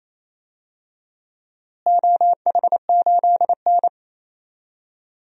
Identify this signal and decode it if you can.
Morse code, heard as a keyed tone, about 28 wpm, 700 Hz. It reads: OH8N
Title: DXCC-100 Call Signs - 4 Characters - Repeated 3x - 28wpm